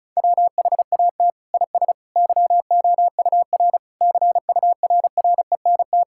Morse code, heard as a keyed tone, about 36 wpm, 700 Hz. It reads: WHAT IS YOUR CURRENT
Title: Sentences from Top 1000 Words - Review Part 6 - 35wpm